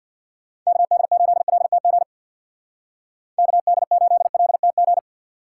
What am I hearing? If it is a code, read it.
Morse code, heard as a keyed tone, about 60 wpm, 700 Hz. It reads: KD8LTG KD8LTG
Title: Speed Racing - USA 2x3 Call Signs 50wpm